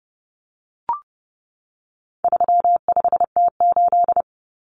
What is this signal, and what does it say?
Morse code, heard as a keyed tone, about 30 wpm, 700 Hz. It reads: 35T8
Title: Common Cut Numbers - 4-Digit 30wpm